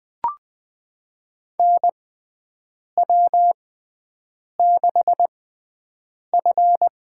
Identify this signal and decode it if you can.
Morse code, heard as a keyed tone, about 20 wpm, spaced as for 7 wpm, 700 Hz. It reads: NW6F
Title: USA Call Signs - 4 Characters - 6x Character Spacing - 20wpm